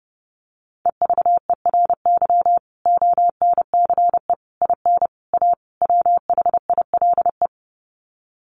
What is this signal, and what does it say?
Morse code, heard as a keyed tone, about 30 wpm, 700 Hz. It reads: EVERY ONCE IN A WHILE
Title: Sets of 5 Words 30wpm